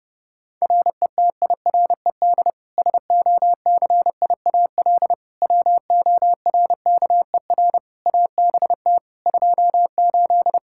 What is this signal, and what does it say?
Morse code, heard as a keyed tone, about 32 wpm, 700 Hz. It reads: RETIRED SOCIAL WORKER ABT 28
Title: K1USN SST QSO Practice 30wpm